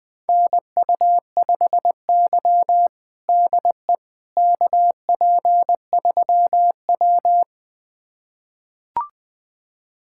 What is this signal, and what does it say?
Morse code, heard as a keyed tone, about 20 wpm, 700 Hz. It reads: NU5Y DE KP3W
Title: USA Callsign DE Callsign - 4 Characters - 20wpm